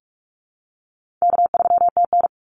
Morse code, heard as a keyed tone, about 44 wpm, 700 Hz. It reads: K3TN
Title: Easy Call Signs Repeated 2x - 45wpm